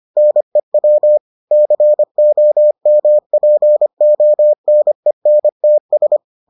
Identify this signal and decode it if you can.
Morse code, heard as a keyed tone, about 25 wpm, 600 Hz. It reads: NEW COMPONENTS